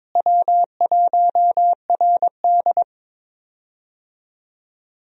Morse code, heard as a keyed tone, about 21 wpm, 700 Hz. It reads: W1RD
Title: USA Call Signs - 4 Characters - Repeated 2x - 22wpm